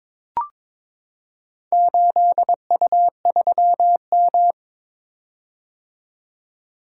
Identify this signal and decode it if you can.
Morse code, heard as a keyed tone, about 21 wpm, 700 Hz. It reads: 8U3M
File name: practice / 22wpm / Four Letter-Number Combinations - 22wpm